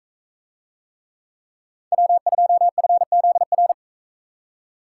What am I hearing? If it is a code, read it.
Morse code, heard as a keyed tone, about 41 wpm, 700 Hz. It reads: W2FZR